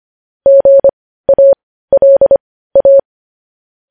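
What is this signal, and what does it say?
Morse code, heard as a keyed tone, about 25 wpm, spaced as for 15 wpm, 550 Hz. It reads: GALA
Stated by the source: Morse code transcription of news headlines.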